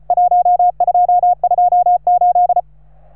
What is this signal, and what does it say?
Morse code, heard as a keyed tone, about 36 wpm, 710 Hz. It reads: 1228
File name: RS3A-20